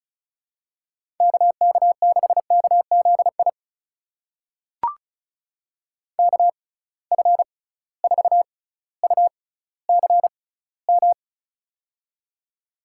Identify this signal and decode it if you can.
Morse code, heard as a keyed tone, about 36 wpm, spaced as for 18 wpm, 700 Hz. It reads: KK6KZI KF4UCM
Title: USA Call Signs - 6x Character Spacing - 35wpm